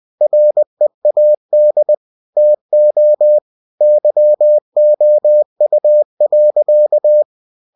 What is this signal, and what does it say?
Morse code, heard as a keyed tone, about 20 wpm, 600 Hz. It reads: READ TO YOU.
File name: cw-books / winnie-the-pooh / winnie-20wpm-page-0019